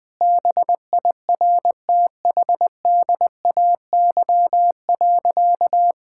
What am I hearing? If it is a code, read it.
Morse code, heard as a keyed tone, about 19 wpm, 700 Hz. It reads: BIRTHDAY.
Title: Sentences with 6-Word Phrases - 20wpm